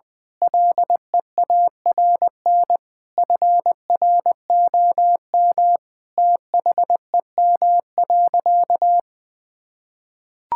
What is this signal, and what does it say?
Morse code, heard as a keyed tone, about 20 wpm, 700 Hz. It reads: LEARN FROM THEM.
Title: Difficult Sentences with Hints - 20wpm